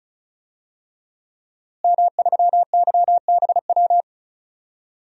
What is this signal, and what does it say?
Morse code, heard as a keyed tone, about 36 wpm, 700 Hz. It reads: M3YBW
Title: DXCC-100 Call Signs - 5 Characters - Repeated 3x - 35wpm